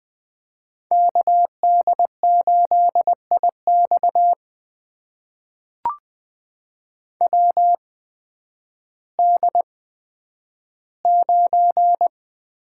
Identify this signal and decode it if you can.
Morse code, heard as a keyed tone, about 21 wpm, spaced as for 8 wpm, 700 Hz. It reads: KD8IXW D9
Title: USA Call Signs - 8x Character Spacing - 20wpm